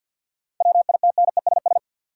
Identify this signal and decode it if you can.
Morse code, heard as a keyed tone, about 52 wpm, 700 Hz. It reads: WITNESS
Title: Seven Letter Words - 50wpm